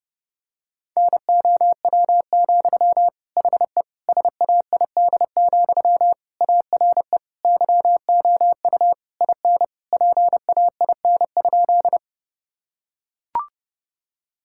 Morse code, heard as a keyed tone, about 30 wpm, 700 Hz. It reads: NOW, HE SAID, ARE YOU IN PAIN?
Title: Sentences from Top 1600 Words 30wpm